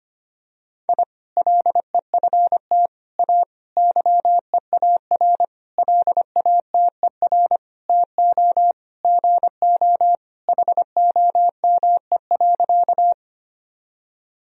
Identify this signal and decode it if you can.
Morse code, heard as a keyed tone, about 25 wpm, 700 Hz. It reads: I LEFT A YEAR LATER TO GO HOME.
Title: Sentences from Top 1000 Words - Review Part 4 - 25wpm